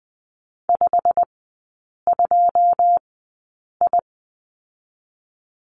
Morse code, heard as a keyed tone, about 20 wpm, spaced as for 8 wpm, 700 Hz. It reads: 52I
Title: 087 Lesson 15e - Number 5 - Three Characters 20wpm